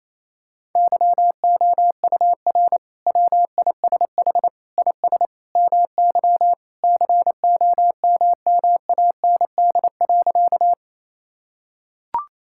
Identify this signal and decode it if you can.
Morse code, heard as a keyed tone, about 28 wpm, 700 Hz. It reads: YOUR WISH IS MY COMMAND.
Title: Sentences from Top 1600 Words 28wpm